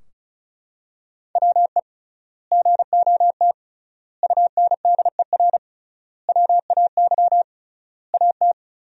Morse code, heard as a keyed tone, about 36 wpm, 700 Hz. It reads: WE GOT UNDER WAY AT
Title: 3x Word-Spacing - Sentences from Top 300 Words - 35wpm